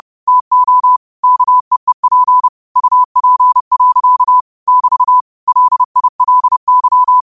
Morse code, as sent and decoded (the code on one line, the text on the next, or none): - --- / -.- . .--. / ..- .--. .-.-.- / -...- / .-.. .. .-.. -.--
TO KEEP UP. <BT> LILY